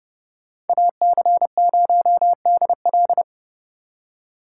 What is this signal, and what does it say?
Morse code, heard as a keyed tone, about 28 wpm, 700 Hz. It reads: AC0DL